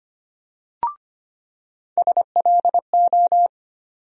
Morse code, heard as a keyed tone, about 25 wpm, 700 Hz. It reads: SLO
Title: USA Call Sign Postfixes - 3 Letters 25wpm